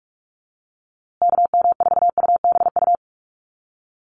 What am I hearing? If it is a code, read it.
Morse code, heard as a keyed tone, about 44 wpm, 690 Hz. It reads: KM4UBU
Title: USA Call Signs Repeated 3x 45wpm